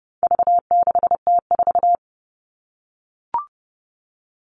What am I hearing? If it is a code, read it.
Morse code, heard as a keyed tone, about 30 wpm, 700 Hz. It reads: V6T4
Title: All Cut Numbers - 4-Digit 30wpm